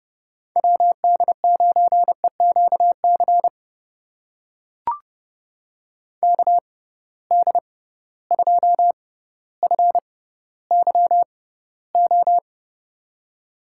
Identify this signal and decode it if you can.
Morse code, heard as a keyed tone, about 30 wpm, spaced as for 15 wpm, 700 Hz. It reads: WD9EQC KD2FYO
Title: USA Call Signs - 6 Characters - 6x Character Spacing - 30wpm